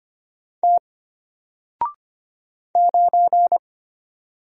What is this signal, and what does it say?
Morse code, heard as a keyed tone, about 25 wpm, 700 Hz. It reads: T 9
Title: Single Letter-Number Alt 25wpm